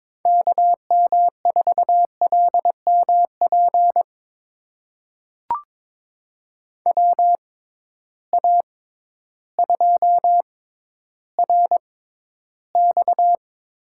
Morse code, heard as a keyed tone, about 22 wpm, spaced as for 12 wpm, 700 Hz. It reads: KM4LMP WA2RX
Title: USA Call Signs - 6 Characters - 6x Character Spacing - 22wpm